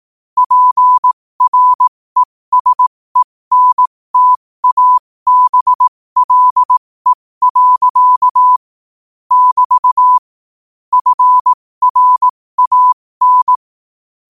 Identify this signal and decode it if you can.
Morse code, heard as a keyed tone, about 18 wpm, spaced as for 15 wpm, 1000 Hz. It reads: PRESENTABLE. <BT> FRAN
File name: qotdfiles / 15wpm